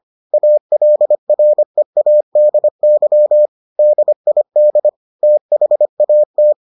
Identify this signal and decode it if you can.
Morse code, heard as a keyed tone, about 25 wpm, 600 Hz. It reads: ALREADY DID THAT